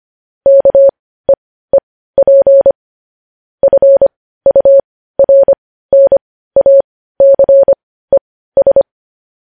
Morse code, heard as a keyed tone, about 25 wpm, spaced as for 15 wpm, 550 Hz. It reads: KEEP FURNACES
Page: news headlines in morse code at 15 wpm